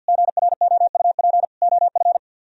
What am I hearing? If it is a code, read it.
Morse code, heard as a keyed tone, about 63 wpm, 700 Hz. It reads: GROUP OF